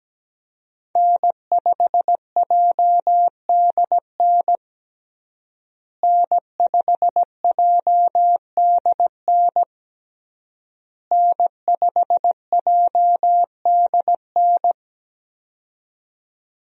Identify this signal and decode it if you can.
Morse code, heard as a keyed tone, about 17 wpm, 700 Hz. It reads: N5JDN N5JDN N5JDN
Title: USA Call Signs - 5 Characters - Repeated 3x - 17wpm